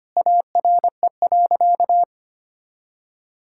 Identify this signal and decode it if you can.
Morse code, heard as a keyed tone, about 25 wpm, 700 Hz. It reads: ARE.
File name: practice / 25wpm / Sentences with 2-Word Phrases - 25wpm